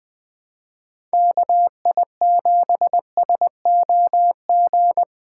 Morse code, heard as a keyed tone, about 23 wpm, 700 Hz. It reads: KI7SOG
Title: Speed Racing - USA 2x3 Call Signs 20wpm